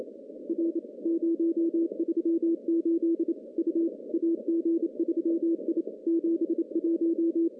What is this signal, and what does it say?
Morse code, heard as a keyed tone, about 28 wpm, 370 Hz. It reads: R 038 U AG3I 71
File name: contests / CW-SS-k5tr- -0000